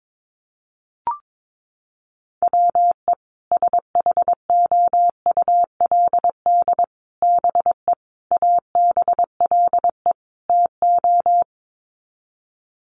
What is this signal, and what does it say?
Morse code, heard as a keyed tone, about 22 wpm, 700 Hz. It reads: WE SHOULD BE ABLE TO
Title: Sets of 5 Words Encore2 22wpm